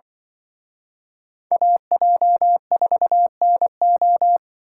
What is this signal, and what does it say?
Morse code, heard as a keyed tone, about 24 wpm, 700 Hz. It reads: AJ4NO
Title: Speed Racing - USA 2x2 Call Signs 20wpm